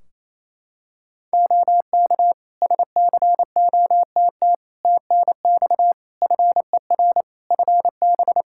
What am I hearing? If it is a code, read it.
Morse code, heard as a keyed tone, about 28 wpm, 700 Hz. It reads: OK SCOTT TNX FER FB
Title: Practice QSO - Exchange 4 - Ending - 28wpm